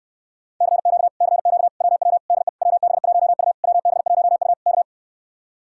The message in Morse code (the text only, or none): CQ CQ WW DE WB1U WB1U K